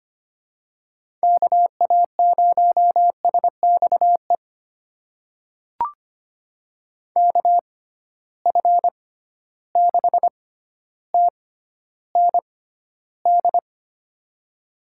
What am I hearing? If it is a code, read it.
Morse code, heard as a keyed tone, about 25 wpm, spaced as for 13 wpm, 700 Hz. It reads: KA0SXE KF6TND
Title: USA Call Signs - 6 Characters - 6x Character Spacing - 25wpm